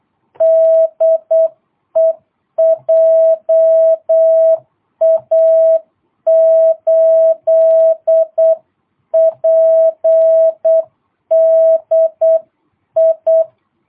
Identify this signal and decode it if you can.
Morse code, heard as a keyed tone, about 8 wpm, 660 Hz. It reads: DE JA8PDI
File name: jh8cbh / 2022cw